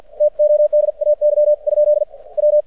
Morse code, heard as a keyed tone, about 48 wpm, 600 Hz. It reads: TODAY ? W
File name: archived / CRSnet